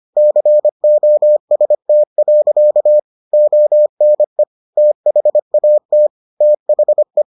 Morse code, heard as a keyed tone, about 25 wpm, 600 Hz. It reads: COST. ONE THAT THE